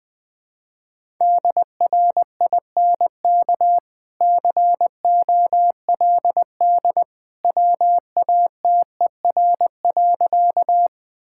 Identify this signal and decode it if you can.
Morse code, heard as a keyed tone, about 20 wpm, 700 Hz. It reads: DRINK COLD WATER.
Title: Easy Sentences with Hints - 20wpm